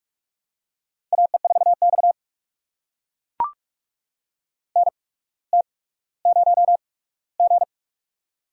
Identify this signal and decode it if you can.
Morse code, heard as a keyed tone, about 44 wpm, spaced as for 19 wpm, 700 Hz. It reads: AE4X NT0G
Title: USA Call Signs - 4 Characters - 8x Character Spacing - 45wpm